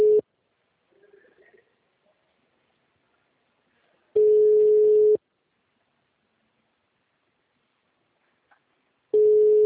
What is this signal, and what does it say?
Ringback tone (Europe, 425 Hz).
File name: out- -2022- - - .35065